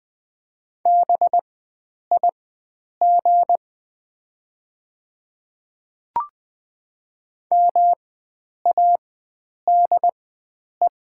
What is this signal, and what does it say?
Morse code, heard as a keyed tone, about 20 wpm, spaced as for 9 wpm, 700 Hz. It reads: BIG MADE
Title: 4x Character Spacing - Top 300 Words - 20wpm